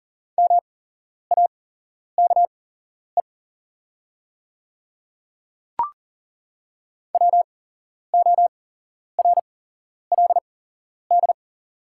Morse code, heard as a keyed tone, about 42 wpm, spaced as for 11 wpm, 700 Hz. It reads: MAKE WORLD
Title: 8x Character Spacing - Top 200 Words - 40wpm